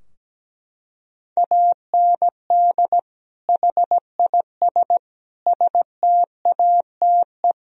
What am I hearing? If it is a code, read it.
Morse code, heard as a keyed tone, about 17 wpm, 700 Hz. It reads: AND HIS STATE
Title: Sentences from Top 1000 Words - Review Part 2 - 17wpm